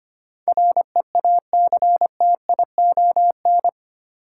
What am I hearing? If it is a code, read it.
Morse code, heard as a keyed tone, about 25 wpm, 700 Hz. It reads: REACTION
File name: 8 Letter Words - First 3 Letters Given - 25wpm